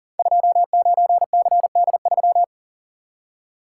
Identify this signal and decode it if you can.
Morse code, heard as a keyed tone, about 39 wpm, 700 Hz. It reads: 29CD3
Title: Code Groups - 5 Letter:Numbers - 40wpm